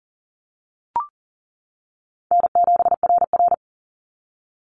Morse code, heard as a keyed tone, about 40 wpm, 700 Hz. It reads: N7RR
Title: NA SOTA Activator Callsigns 40wpm